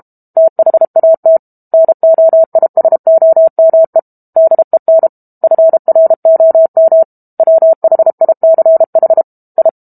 Morse code, heard as a keyed tone, about 32 wpm, 650 Hz. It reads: THAT NOISOME DEN FROM WHICH I
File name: Book Two / 06-Chapter 6 - 32 WPM